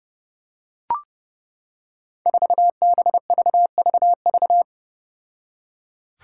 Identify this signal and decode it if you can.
Morse code, heard as a keyed tone, about 30 wpm, 700 Hz. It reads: 4BVVV
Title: Five Challenging Letter-Number Combinations 30wpm